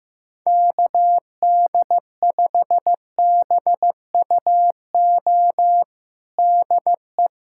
Morse code, heard as a keyed tone, about 15 wpm, 700 Hz. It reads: KD5BUO DE
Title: USA Callsign DE Callsign - 6 Characters - 15wpm